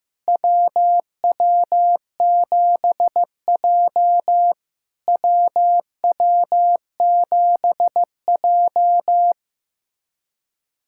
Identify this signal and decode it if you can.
Morse code, heard as a keyed tone, about 15 wpm, 700 Hz. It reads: WW7J WW7J
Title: SOTA QSO Practice 15wpm